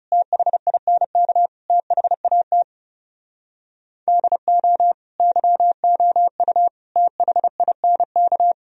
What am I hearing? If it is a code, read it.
Morse code, heard as a keyed tone, about 37 wpm, 700 Hz. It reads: THINK THAT DO YOU THINK
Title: Speed-Racing - Sets of 4 Words 30wpm